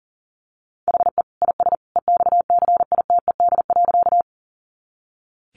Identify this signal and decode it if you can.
Morse code, heard as a keyed tone, about 40 wpm, 700 Hz. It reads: HE IS EXCITED.
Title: Additional Easy Phrases 40wpm